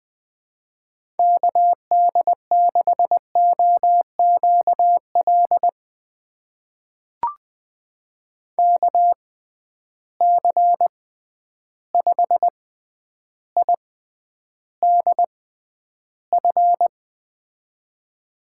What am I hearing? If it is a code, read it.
Morse code, heard as a keyed tone, about 20 wpm, spaced as for 10 wpm, 700 Hz. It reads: KD6OQL KC5IDF
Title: USA Call Signs - 6 Characters - 6x Character Spacing - 20wpm